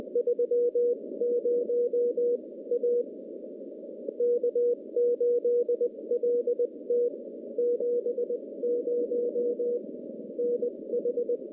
Morse code, heard as a keyed tone, about 20 wpm, spaced as for 17 wpm, 480 Hz. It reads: 30 A K8LT 70 NH
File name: contests / CW-SS-k5tr- -0200